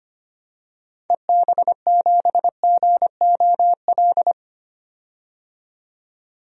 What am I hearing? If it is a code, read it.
Morse code, heard as a keyed tone, about 25 wpm, 700 Hz. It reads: EB7GOL
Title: DXCC-100 Call Signs - 6 Characters - 25wpm